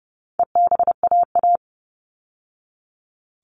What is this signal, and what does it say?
Morse code, heard as a keyed tone, about 30 wpm, 700 Hz. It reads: EBAA
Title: All Cut Numbers - 4-Digit 30wpm